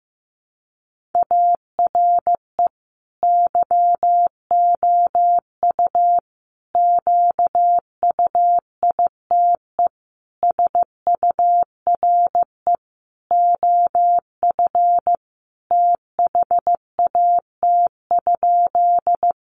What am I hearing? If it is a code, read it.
Morse code, heard as a keyed tone, about 15 wpm, 700 Hz. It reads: ARE YOU QUITE SURE OF THAT?
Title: Sentences from Top 700 Words 15wpm